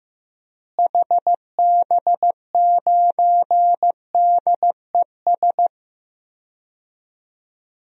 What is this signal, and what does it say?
Morse code, heard as a keyed tone, about 16 wpm, 700 Hz. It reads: HB9DES
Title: DXCC-100 Call Signs - 6 Characters - Repeated 2x - 15wpm